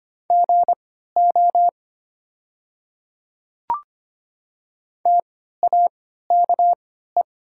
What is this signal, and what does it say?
Morse code, heard as a keyed tone, about 25 wpm, spaced as for 14 wpm, 700 Hz. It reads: GO TAKE
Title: 3x Character Spacing - Top 300 Words - 25wpm